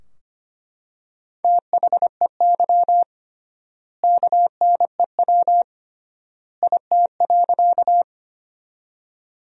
Morse code, heard as a keyed tone, about 25 wpm, 700 Hz. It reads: THEY KNEW IT.
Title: 3x Word-Spacing - Sentences from Top 400 Words - 25wpm